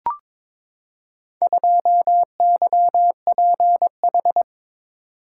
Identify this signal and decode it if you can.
Morse code, heard as a keyed tone, about 22 wpm, 700 Hz. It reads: 2YPH
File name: practice / 22wpm / Four Letter-Number Combinations - 22wpm